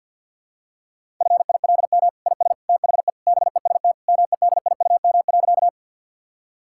Morse code, heard as a keyed tone, about 50 wpm, 700 Hz. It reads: FILM IS THE BEST MEDIUM.
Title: Sentences from Top 1000 Words - Review Part 6 - 50wpm